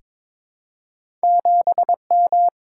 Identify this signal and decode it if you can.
Morse code, heard as a keyed tone, about 22 wpm, 700 Hz. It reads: 7M
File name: practice / 22wpm / Two Letter-Number Combinations - 22wpm